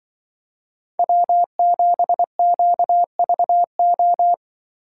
Morse code, heard as a keyed tone, about 22 wpm, 700 Hz. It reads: W7QVO
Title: Speed Racing - USA 1x3 Call Signs 20wpm